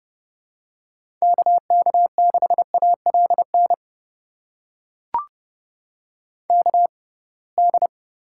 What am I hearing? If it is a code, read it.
Morse code, heard as a keyed tone, about 30 wpm, spaced as for 23 wpm, 700 Hz. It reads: KK6ALN KD
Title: USA Call Signs - 6 Characters - 6x Character Spacing - 30wpm